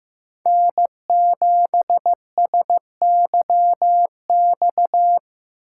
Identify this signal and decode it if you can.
Morse code, heard as a keyed tone, about 18 wpm, 700 Hz. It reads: N7SYX
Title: Speed Racing - USA 1x3 Call Signs 15wpm